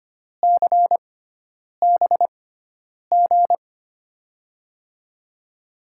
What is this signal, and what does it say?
Morse code, heard as a keyed tone, about 25 wpm, spaced as for 9 wpm, 700 Hz. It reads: CBG
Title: USA Call Signs - 6 Characters - 6x Character Spacing - 25wpm